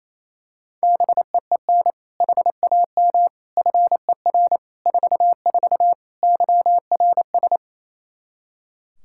Morse code, heard as a keyed tone, about 28 wpm, 700 Hz. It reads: BEEN HAM FER 44 YRS
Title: Practice QSO - Exchange 3 - More Info - 28wpm